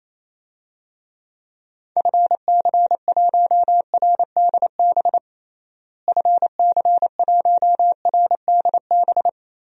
Morse code, heard as a keyed tone, about 28 wpm, 700 Hz. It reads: FC1RDB FC1RDB
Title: DXCC-100 Call Signs - 6 Characters - Repeated 3x - 28wpm